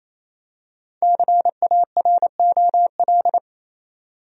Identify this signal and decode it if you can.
Morse code, heard as a keyed tone, about 28 wpm, 700 Hz. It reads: CAROL
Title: USA Common Names - 28wpm